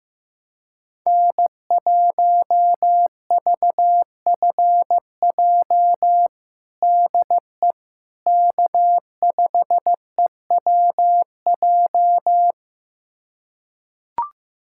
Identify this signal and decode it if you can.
Morse code, heard as a keyed tone, about 16 wpm, 700 Hz. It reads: N1VFJ DE K5EWJ
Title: USA Callsign DE Callsign - 5 Characters - 15wpm